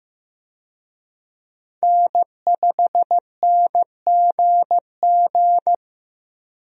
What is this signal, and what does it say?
Morse code, heard as a keyed tone, about 15 wpm, 700 Hz. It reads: N5NGG
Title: USA Call Signs - 5 Characters - Repeated 3x - 15wpm